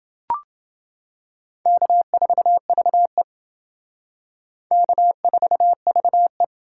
Morse code, heard as a keyed tone, about 28 wpm, 700 Hz. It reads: K4VE K4VE